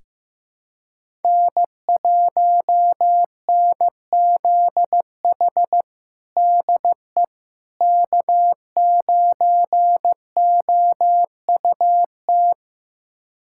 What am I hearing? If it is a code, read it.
Morse code, heard as a keyed tone, about 15 wpm, 700 Hz. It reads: N1NZH DE K9OUT
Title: USA Callsign DE Callsign - 5 Characters - 15wpm